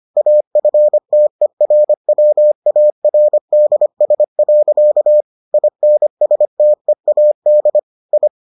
Morse code, heard as a keyed tone, about 25 wpm, 600 Hz. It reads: AFTERWARDS. INSTEAD I